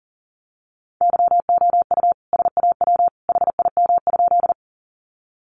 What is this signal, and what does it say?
Morse code, heard as a keyed tone, about 40 wpm, 700 Hz. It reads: YOU SAW HIM?
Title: Sentences from Top 300 Words 40wpm